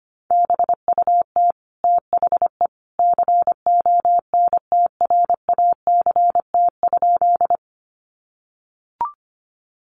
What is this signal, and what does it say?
Morse code, heard as a keyed tone, about 25 wpm, 700 Hz. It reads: BUT THE CONTRACT?
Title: Sentences from Top 1400 Words 25wpm